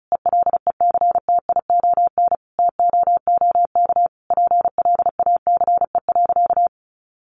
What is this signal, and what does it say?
Morse code, heard as a keyed tone, about 34 wpm, 700 Hz. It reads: ELECTION TOOK PLACE.
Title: Sentences from Top 1500 Words 35wpm